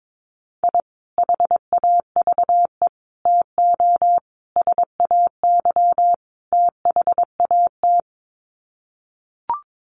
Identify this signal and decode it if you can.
Morse code, heard as a keyed tone, about 22 wpm, 700 Hz. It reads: I HAVE TO SAY THAT
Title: Sets of 5 Words Encore 22wpm